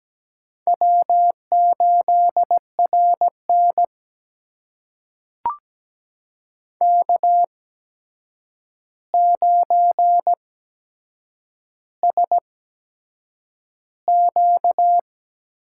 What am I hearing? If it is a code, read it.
Morse code, heard as a keyed tone, about 17 wpm, spaced as for 7 wpm, 700 Hz. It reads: W8RN K9SQ
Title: USA Call Signs - 4 Characters - 8x Character Spacing - 17wpm